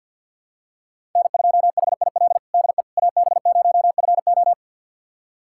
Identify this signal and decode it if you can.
Morse code, heard as a keyed tone, about 53 wpm, 700 Hz. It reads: N2HIL DE AB0FO
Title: USA Callsign DE Callsign - 5 Characters - 50wpm